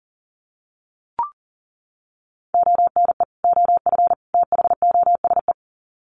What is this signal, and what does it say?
Morse code, heard as a keyed tone, about 40 wpm, 700 Hz. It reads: ONE OF THOSE
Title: Sets of 3 Words 40wpm